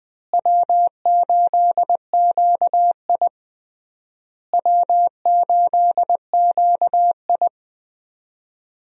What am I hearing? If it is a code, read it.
Morse code, heard as a keyed tone, about 20 wpm, 700 Hz. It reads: W8QI W8QI
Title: USA Call Signs - 4 Characters - Repeated 3x - 20wpm